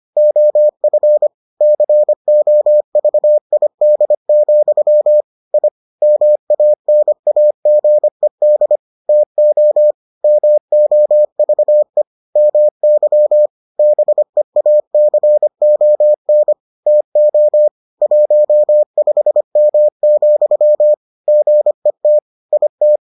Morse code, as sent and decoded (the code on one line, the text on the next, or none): --- ..-. / -.-. --- ...- .. -.. --..-- / .. / -- .- -. .- --. -.. / - --- / -- --- ...- . / -- -.-- / -... .- -.-. --- -. / - --- / .---- ..... -- --..-- / --. - / .. -
OF COVID, I MANAGED TO MOVE MY BEACON TO 15M, GET IT